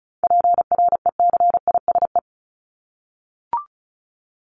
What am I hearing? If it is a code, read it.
Morse code, heard as a keyed tone, about 34 wpm, 700 Hz. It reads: PRECISE
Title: LY Suffix - Top 100 Words with Base Words 35wpm